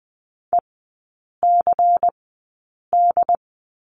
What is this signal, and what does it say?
Morse code, heard as a keyed tone, about 20 wpm, spaced as for 8 wpm, 700 Hz. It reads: ECD
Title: 127 Lesson 21e - Number 6 - Three Characters 20wpm